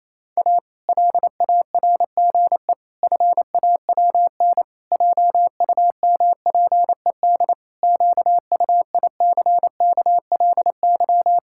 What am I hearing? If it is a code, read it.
Morse code, heard as a keyed tone, about 28 wpm, 700 Hz. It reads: A LARGE FAWN JUMPED QUICKLY
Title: Pangram Challenge v2 - 28wpm